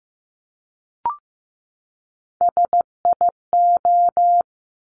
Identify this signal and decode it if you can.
Morse code, heard as a keyed tone, about 15 wpm, 700 Hz. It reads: SIO
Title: Top Three Letter Combinations in Words 15wpm